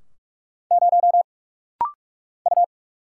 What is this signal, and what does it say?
Morse code, heard as a keyed tone, about 44 wpm, 690 Hz. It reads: 0 U
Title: Single Letter-Number - Rapid-Fire 45wpm